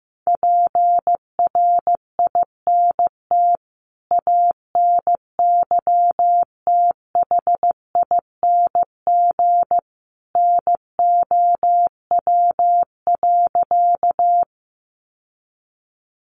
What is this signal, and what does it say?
Morse code, heard as a keyed tone, about 15 wpm, 700 Hz. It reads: PRINT ANYTHING NOW.
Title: Sentences from Top 1400 Words 15wpm